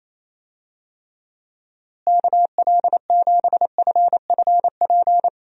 Morse code, heard as a keyed tone, about 28 wpm, 700 Hz. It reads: KL7FFP
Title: USA Call Signs - 6 Characters - Repeated 3x - 28wpm